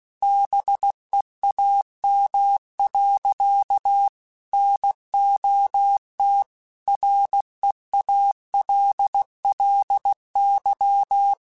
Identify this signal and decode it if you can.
Morse code, heard as a keyed tone, about 16 wpm, 800 Hz. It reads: BEAM. NOT REALLY
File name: QSOs / QSOs-mp3s / WD0BGP 1@15